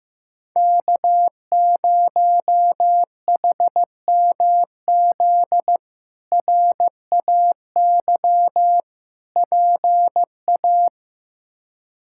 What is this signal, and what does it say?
Morse code, heard as a keyed tone, about 15 wpm, 700 Hz. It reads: K0HMZ RAY PA
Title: K1USN SST QSO Practice - 15wpm